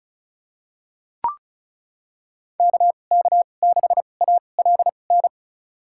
Morse code, heard as a keyed tone, about 34 wpm, spaced as for 25 wpm, 700 Hz. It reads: KK6ALN
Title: USA Call Signs - 6 Characters - 2x Character Spacing - 35wpm